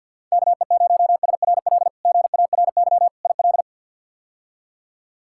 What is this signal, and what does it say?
Morse code, heard as a keyed tone, about 51 wpm, 700 Hz. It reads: KE0SRL GARY IL
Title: K1USN SST QSO Practice - 50wpm